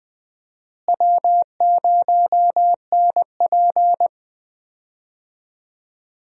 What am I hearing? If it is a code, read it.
Morse code, heard as a keyed tone, about 20 wpm, 700 Hz. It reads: W0NP
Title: USA Call Signs - 4 Characters - Repeated 2x - 20wpm